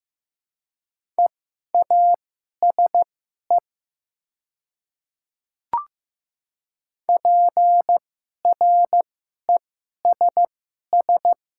Morse code, heard as a keyed tone, about 15 wpm, spaced as for 11 wpm, 700 Hz. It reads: EASE PRESS
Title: 2x Character Spacing - Top 500 Words - 15wpm